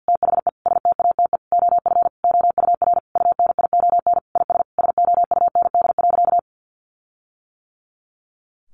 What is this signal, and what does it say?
Morse code, heard as a keyed tone, about 50 wpm, 700 Hz. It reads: THE STATE OF OUR UNION IS SOUND.
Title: Sentences from Top 1400 Words 50wpm